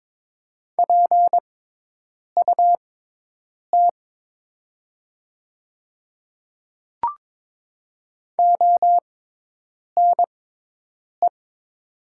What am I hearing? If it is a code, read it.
Morse code, heard as a keyed tone, about 22 wpm, spaced as for 8 wpm, 700 Hz. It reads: PUT ONE
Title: 6x Character Spacing - Top 200 Words - 22wpm